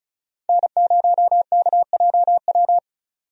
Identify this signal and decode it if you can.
Morse code, heard as a keyed tone, about 34 wpm, 700 Hz. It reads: N0KJW
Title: USA Call Signs - 5 Characters - 35wpm